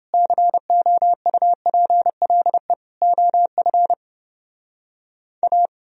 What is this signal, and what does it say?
Morse code, heard as a keyed tone, about 30 wpm, 700 Hz. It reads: COUPLE OF A